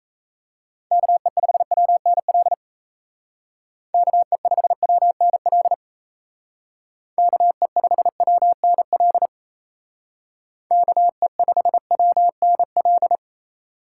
Morse code, heard as a keyed tone, about 36 wpm, 700 Hz. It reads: KE5WNL KE5WNL KE5WNL KE5WNL